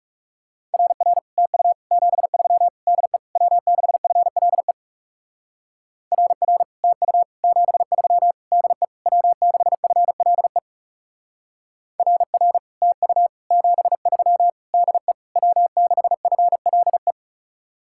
Morse code, heard as a keyed tone, about 37 wpm, 700 Hz. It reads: RR TU 73 DE W6FLE RR TU 73 DE W6FLE RR TU 73 DE W6FLE
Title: Speed-Racing Common QSO Phrases - 30wpm